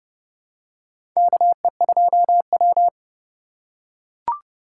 Morse code, heard as a keyed tone, about 30 wpm, 700 Hz. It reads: KE2W
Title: USA Call Signs - 3x Character Spacing - 30wpm